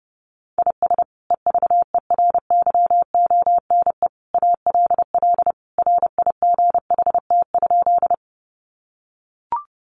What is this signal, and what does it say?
Morse code, heard as a keyed tone, about 30 wpm, 700 Hz. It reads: IS EVERYONE ALL RIGHT?
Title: Sentences from Top 800 Words 30wpm